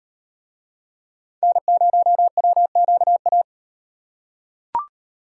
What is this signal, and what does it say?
Morse code, heard as a keyed tone, about 38 wpm, 700 Hz. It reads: N0WQA